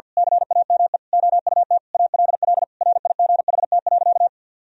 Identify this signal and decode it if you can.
Morse code, heard as a keyed tone, about 50 wpm, 700 Hz. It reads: CAME OUT ALL RIGHT.
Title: Sentences from Top 1000 Words - Review Part 4 - 50wpm